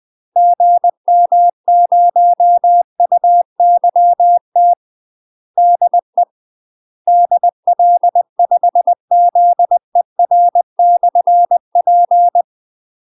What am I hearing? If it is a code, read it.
Morse code, heard as a keyed tone, about 20 wpm, 700 Hz. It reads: GM0UYT DE DL5ZER/P
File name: UK MASTER QSOs TRAINING / 20WPM / UK MASTER QSOs 0043